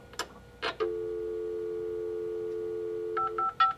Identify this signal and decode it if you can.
Dial tone (North America), 350 + 440 Hz.
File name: zvuk-faksa